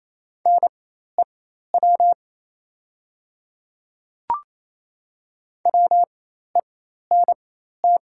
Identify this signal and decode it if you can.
Morse code, heard as a keyed tone, about 28 wpm, spaced as for 13 wpm, 700 Hz. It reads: NEW WENT
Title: 4x Character Spacing - Top 200 Words - 28wpm